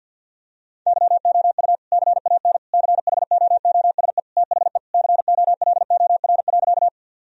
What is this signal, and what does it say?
Morse code, heard as a keyed tone, about 50 wpm, 700 Hz. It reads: YOU CAN CHOOSE THE COLOR.
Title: Sentences from Top 1000 Words - Review Part 4 - 50wpm